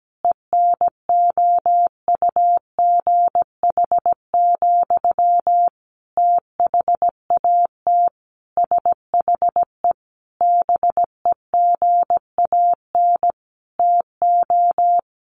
Morse code, heard as a keyed tone, about 17 wpm, 700 Hz. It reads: ENOUGH, THAT SHE BEGAN TO
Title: Sentences from Top 800 Words 17wpm